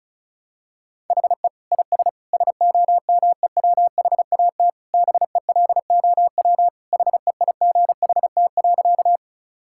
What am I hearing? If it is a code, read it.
Morse code, heard as a keyed tone, about 36 wpm, 700 Hz. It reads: HE IS SOMEWHAT BELOW HEIGHT.
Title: Sentences from Top 1600 Words 35wpm